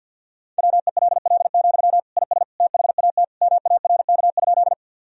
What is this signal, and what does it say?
Morse code, heard as a keyed tone, about 48 wpm, 700 Hz. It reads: WELL, IS THAT MARK?
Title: Sentences from Top 1000 Words - Review Part 1 - 50wpm